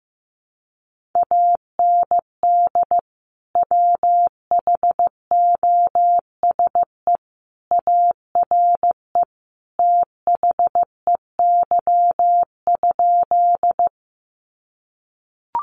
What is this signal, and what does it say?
Morse code, heard as a keyed tone, about 15 wpm, 700 Hz. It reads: AND WHOSE ARE THEY?
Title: Sentences from Top 1200 Words 15wpm